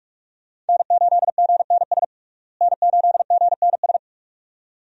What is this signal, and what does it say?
Morse code, heard as a keyed tone, about 47 wpm, 700 Hz. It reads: N8GNS N8GNS
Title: USA Call Signs - 5 Characters - Repeated 2x - 45wpm